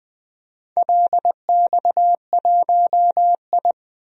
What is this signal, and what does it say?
Morse code, heard as a keyed tone, about 20 wpm, 700 Hz. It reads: LX1I
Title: DXCC-100 Call Signs - 4 Characters - 20wpm